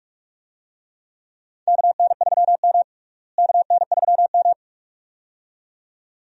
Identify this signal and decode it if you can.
Morse code, heard as a keyed tone, about 46 wpm, 700 Hz. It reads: KN3M KN3M
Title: USA Call Signs - 4 Characters - Repeated 2x - 45wpm